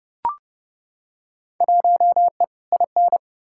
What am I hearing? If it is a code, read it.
Morse code, heard as a keyed tone, about 30 wpm, 700 Hz. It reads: 1E IN
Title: Field Day Exchanges - 30wpm